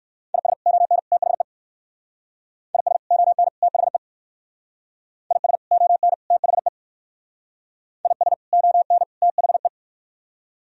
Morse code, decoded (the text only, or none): IS ON THE IS ON THE IS ON THE IS ON THE